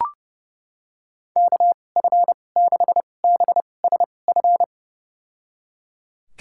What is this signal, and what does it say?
Morse code, heard as a keyed tone, about 32 wpm, spaced as for 22 wpm, 700 Hz. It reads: KF6BSF
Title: USA Call Signs - 6 Characters - 2x Character Spacing - 30wpm